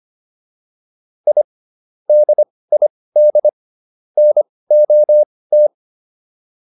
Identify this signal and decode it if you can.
Morse code, heard as a keyed tone, about 25 wpm, spaced as for 18 wpm, 600 Hz. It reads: I DID NOT